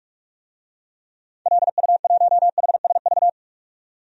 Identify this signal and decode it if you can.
Morse code, heard as a keyed tone, about 42 wpm, 700 Hz. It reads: LU1HSV
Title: DXCC-100 Call Signs - 6 Characters - 45wpm